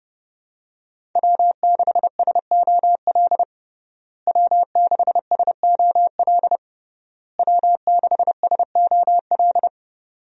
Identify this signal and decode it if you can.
Morse code, heard as a keyed tone, about 30 wpm, 700 Hz. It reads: W6SOL W6SOL W6SOL
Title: USA Call Signs - 5 Characters - Repeated 3x - 30wpm